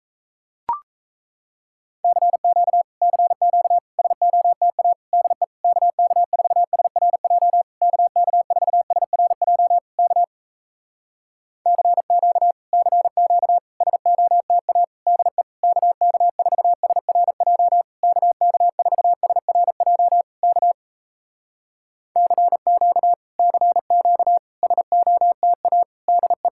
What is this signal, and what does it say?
Morse code, heard as a keyed tone, about 38 wpm, 700 Hz. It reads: CQ CQ SOTA DE KK4SRJ KK4SRJ K CQ CQ SOTA DE KK4SRJ KK4SRJ K CQ CQ SOTA DE